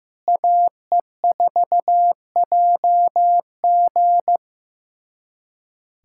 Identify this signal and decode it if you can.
Morse code, heard as a keyed tone, about 15 wpm, 700 Hz. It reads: AE4JG
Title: USA Callsign DE Callsign - 5 Characters - 15wpm